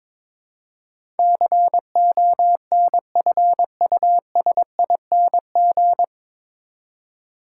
Morse code, heard as a keyed tone, about 23 wpm, 700 Hz. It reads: CONFUSING
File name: 9 Letter Words - First 3 Letters Given - 22wpm